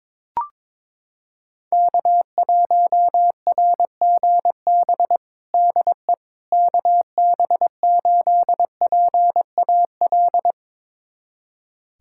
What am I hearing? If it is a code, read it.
Morse code, heard as a keyed tone, about 21 wpm, 700 Hz. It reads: K1RGB DE KB8PAL
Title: USA Callsign DE Callsign 22wpm